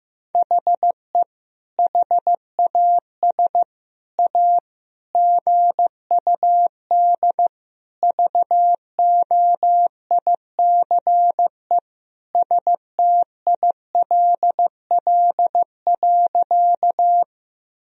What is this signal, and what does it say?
Morse code, heard as a keyed tone, about 15 wpm, 700 Hz. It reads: HE HAS A GUD VOICE STILL.
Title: Sentences from Top 500 Words and Abbreviations 15wpm